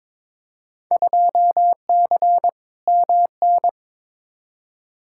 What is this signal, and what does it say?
Morse code, heard as a keyed tone, about 22 wpm, 700 Hz. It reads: 2C MN
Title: Field Day Exchanges Encore - 22wpm